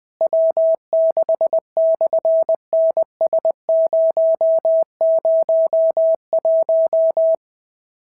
Morse code, heard as a keyed tone, about 20 wpm, 650 Hz. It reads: W6/NS001
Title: SOTA QSO Practice 20wpm